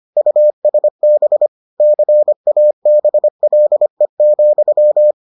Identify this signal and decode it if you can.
Morse code, heard as a keyed tone, about 25 wpm, 600 Hz. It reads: USB CABLE,